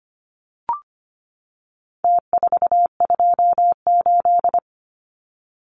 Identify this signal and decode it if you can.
Morse code, heard as a keyed tone, about 25 wpm, 700 Hz. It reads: T428
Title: Numbers - Contest Style - 4-Digit 25wpm